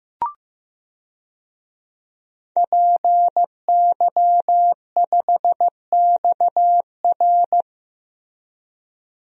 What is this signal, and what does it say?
Morse code, heard as a keyed tone, about 15 wpm, 700 Hz. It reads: PY5XR
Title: DXCC-100 Call Signs - 5 Characters - 15wpm